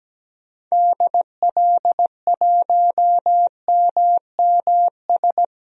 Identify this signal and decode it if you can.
Morse code, heard as a keyed tone, about 17 wpm, 700 Hz. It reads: DL1MMS
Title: DXCC-100 Call Signs - 6 Characters - 17wpm